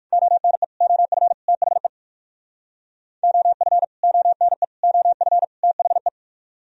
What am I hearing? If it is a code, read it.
Morse code, heard as a keyed tone, about 48 wpm, 700 Hz. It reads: ONE OF THE OF ONE OF THE
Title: Speed-Racing - Sets of 4 Words 45wpm